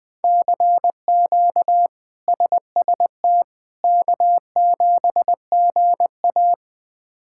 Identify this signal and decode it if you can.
Morse code, heard as a keyed tone, about 20 wpm, 700 Hz. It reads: CQ SST K7GA
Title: K1USN SST QSO Practice - 20wpm